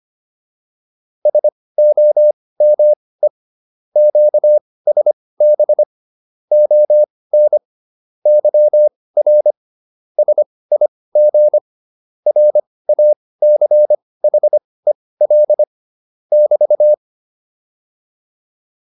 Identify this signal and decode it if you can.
Morse code, heard as a keyed tone, about 25 wpm, spaced as for 18 wpm, 600 Hz. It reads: SOME QSB ON YR SIG RACHEL <BT>